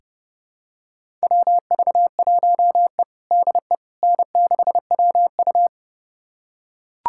Morse code, heard as a keyed tone, about 30 wpm, 700 Hz. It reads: WV1E DE N6WU
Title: USA Callsign DE Callsign - 4 Characters - 30wpm